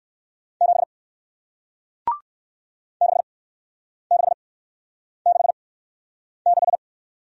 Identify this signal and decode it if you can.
Morse code, heard as a keyed tone, about 57 wpm, spaced as for 9 wpm, 700 Hz. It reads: 7 6666